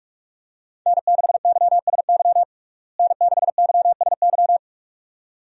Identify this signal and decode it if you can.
Morse code, heard as a keyed tone, about 46 wpm, 700 Hz. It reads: N6YSY N6YSY
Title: USA Call Signs - 5 Characters - Repeated 2x - 45wpm